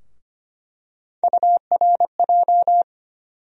Morse code, heard as a keyed tone, about 25 wpm, 700 Hz. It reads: URJ
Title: USA Call Sign Postfixes - 3 Letters 25wpm